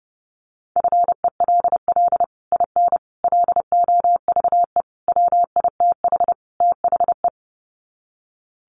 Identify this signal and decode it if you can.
Morse code, heard as a keyed tone, about 30 wpm, 700 Hz. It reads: FELL IN LOVE WITH THE
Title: Sets of 5 Words Encore2 30wpm